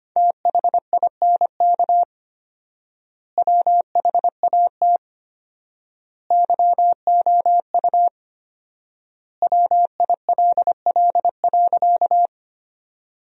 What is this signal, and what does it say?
Morse code, heard as a keyed tone, about 25 wpm, 700 Hz. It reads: THINK WHAT YOU WILL.
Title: 4x Word-Spacing - Sentences from Top 200 Words - 25wpm